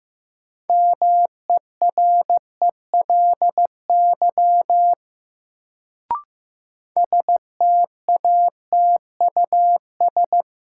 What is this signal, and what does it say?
Morse code, heard as a keyed tone, about 15 wpm, 700 Hz. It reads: MERELY STATUS
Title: Sentences from Top 1400 Words - New Word Drill Alt - 15wpm